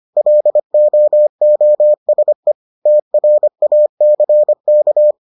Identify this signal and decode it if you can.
Morse code, heard as a keyed tone, about 25 wpm, 600 Hz. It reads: LOOSE TRACK